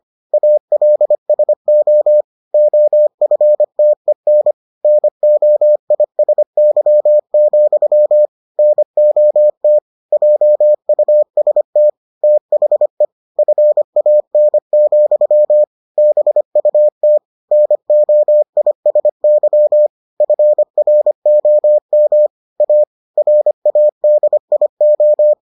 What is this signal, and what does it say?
Morse code, heard as a keyed tone, about 25 wpm, 600 Hz. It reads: ALSO OFTEN NOISY, NOT JUST THE FAN, BUT NOISY FROM A RADIO